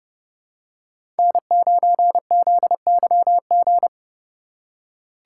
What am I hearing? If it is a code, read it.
Morse code, heard as a keyed tone, about 30 wpm, 700 Hz. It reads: N9ZYG
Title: USA Call Signs - 6x Character Spacing - 30wpm